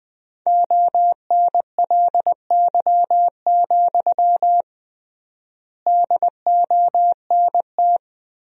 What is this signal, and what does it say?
Morse code, heard as a keyed tone, about 21 wpm, 700 Hz. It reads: ONLY, DONT
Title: 3x Word-Spacing - Sentences from Top 300 Words - 20wpm